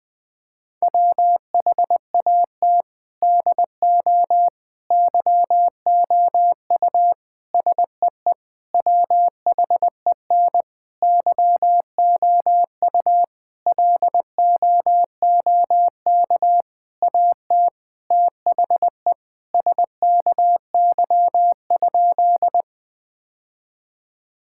Morse code, heard as a keyed tone, about 20 wpm, 700 Hz. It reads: WHAT DO YOU SEE WHEN YOU LOOK AT THE SKY?
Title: Sentences with 4-Word Phrases - 20wpm